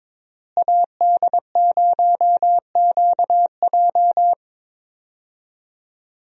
Morse code, heard as a keyed tone, about 22 wpm, 700 Hz. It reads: AD0QJ
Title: USA Call Signs - 5 Characters - Repeated 2x - 22wpm